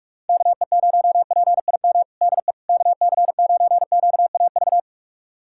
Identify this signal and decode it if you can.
Morse code, heard as a keyed tone, about 41 wpm, 700 Hz. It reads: KE0PIM DE KC9QAV
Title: USA Callsign DE Callsign - 6 Characters - 45wpm